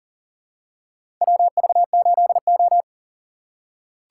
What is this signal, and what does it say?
Morse code, heard as a keyed tone, about 40 wpm, 700 Hz. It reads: WV8O
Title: USA Call Signs - 4 Characters - 6x Character Spacing - 40wpm